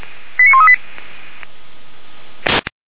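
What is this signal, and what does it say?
Five-tone selective call (ZVEI), heard as quick runs of five tones, tones of about 70 ms. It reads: 89138
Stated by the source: ZVEI five-tone selective call 89200 received by SDR, from Feuerwehr Orlamünde.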